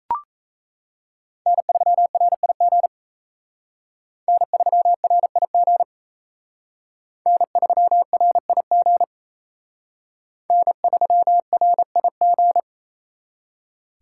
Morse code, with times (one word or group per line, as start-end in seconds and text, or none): 1.5-2.9 N3RIG
4.3-5.8 N3RIG
7.3-9.0 N3RIG
10.5-12.6 N3RIG